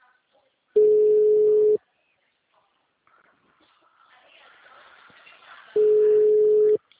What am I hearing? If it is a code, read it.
Ringback tone (Europe, 425 Hz), with speech.
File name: out- -2016- - - .21523